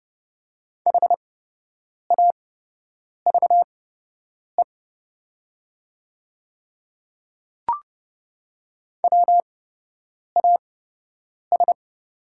Morse code, heard as a keyed tone, about 30 wpm, spaced as for 8 wpm, 700 Hz. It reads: HAVE WAS
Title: 8x Character Spacing - Top 300 Words - 30wpm